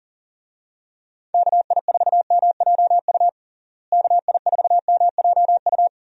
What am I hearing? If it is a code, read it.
Morse code, heard as a keyed tone, about 40 wpm, 700 Hz. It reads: KI4MJU KI4MJU
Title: USA Call Signs - 6 Characters - Repeated 3x - 40wpm